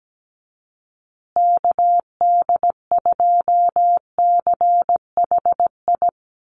Morse code, heard as a keyed tone, about 17 wpm, 700 Hz. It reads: KD2CHI
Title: USA Call Signs Repeated 2x 17wpm